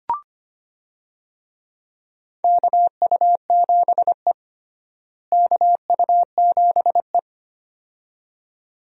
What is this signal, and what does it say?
Morse code, heard as a keyed tone, about 25 wpm, 700 Hz. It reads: KU7E KU7E
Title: USA Call Signs - 4 Characters - Repeated 2x - 25wpm